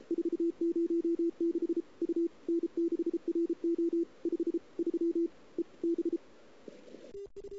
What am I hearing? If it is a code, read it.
Morse code, heard as a keyed tone, about 33 wpm, 340 Hz. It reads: 406 U N6RO 53 EB
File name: CW / 2008-ss-cw-k5tr-op-n5rz -02:00